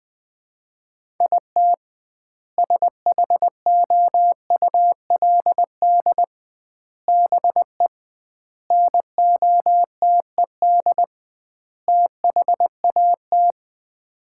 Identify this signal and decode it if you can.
Morse code, heard as a keyed tone, about 20 wpm, 700 Hz. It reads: IT SHOULD BE NOTED THAT
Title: 2x Word-Spacing - Sets of 5 Words - 20wpm